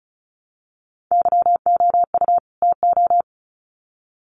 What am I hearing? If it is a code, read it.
Morse code, heard as a keyed tone, about 34 wpm, 700 Hz. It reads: YOU TO
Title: Sets of 2 Words 35wpm